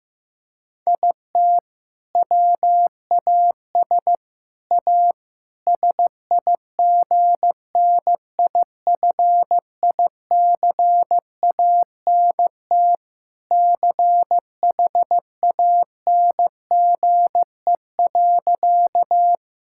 Morse code, heard as a keyed tone, about 15 wpm, 700 Hz. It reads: IT WAS A SIGNIFICANT CHANGE.
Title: Sentences from Top 1000 Words - Review Part 6 - 15wpm